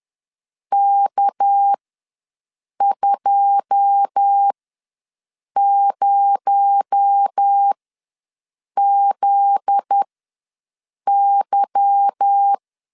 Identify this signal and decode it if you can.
Morse code, heard as a keyed tone, about 10 wpm, spaced as for 6 wpm, 800 Hz. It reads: K20ZY
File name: LEZIONI / 10 SIGNS / 10A SLASH ws 10wpm 4121